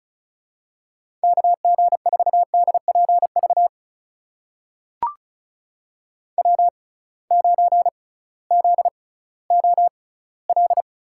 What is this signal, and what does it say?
Morse code, heard as a keyed tone, about 36 wpm, spaced as for 19 wpm, 700 Hz. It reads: KG4DPV W9ZOL
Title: USA Call Signs - 6x Character Spacing - 35wpm